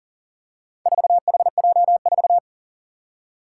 Morse code, heard as a keyed tone, about 40 wpm, 700 Hz. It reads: 4HJ4
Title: Four Challenging Letter-Number Combinations 40wpm